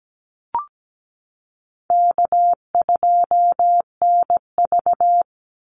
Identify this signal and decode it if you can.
Morse code, heard as a keyed tone, about 17 wpm, 700 Hz. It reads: K2NV
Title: Easy Call Signs Repeated 3x - 17wpm